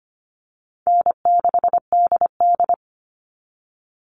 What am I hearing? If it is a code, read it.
Morse code, heard as a keyed tone, about 25 wpm, 700 Hz. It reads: N6DD
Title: All Cut Numbers - 4-Digit 25wpm